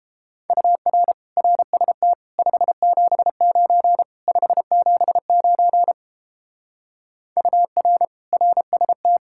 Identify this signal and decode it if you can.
Morse code, heard as a keyed tone, about 32 wpm, 700 Hz. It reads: UR RST 579 579 UR RST